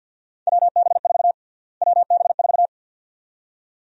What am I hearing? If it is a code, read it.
Morse code, heard as a keyed tone, about 51 wpm, 700 Hz. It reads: WB4 WB4
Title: USA Call Sign Prefixes - Repeated 2x - 50wpm